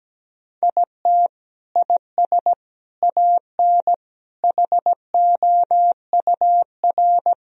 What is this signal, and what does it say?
Morse code, heard as a keyed tone, about 17 wpm, 700 Hz. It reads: IT IS AN HOUR
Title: Sentences from Top 1000 Words - Review Part 2 - 17wpm